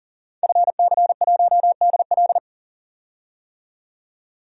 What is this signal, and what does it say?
Morse code, heard as a keyed tone, about 40 wpm, 700 Hz. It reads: FC1DL
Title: DXCC-100 Call Signs - 5 Characters - 40wpm